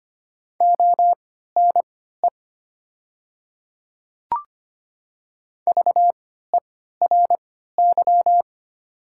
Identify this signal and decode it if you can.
Morse code, heard as a keyed tone, about 25 wpm, spaced as for 14 wpm, 700 Hz. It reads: ONE VERY
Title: 3x Character Spacing - Top 200 Words - 25wpm